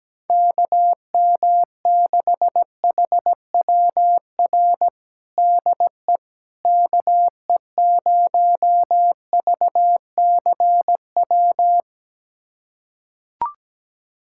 Morse code, heard as a keyed tone, about 17 wpm, 700 Hz. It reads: KM6HWR DE KE0VCW
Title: USA Callsign DE Callsign - 6 Characters - 17wpm